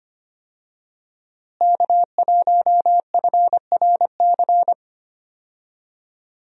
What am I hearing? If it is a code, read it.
Morse code, heard as a keyed tone, about 25 wpm, 700 Hz. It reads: K1FRC
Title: USA Call Signs - 5 Characters - Repeated 2x - 25wpm